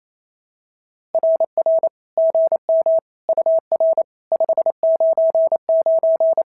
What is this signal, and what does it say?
Morse code, heard as a keyed tone, about 25 wpm, 650 Hz. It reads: RR GM UR 599
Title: SOTA QSO Practice 28wpm